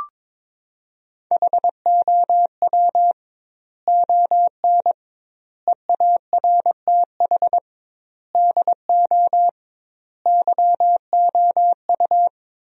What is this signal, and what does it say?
Morse code, heard as a keyed tone, about 22 wpm, 700 Hz. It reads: HOW ON EARTH DO YOU
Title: 2x Word-Spacing - Sentences from Top 200 Words - 22wpm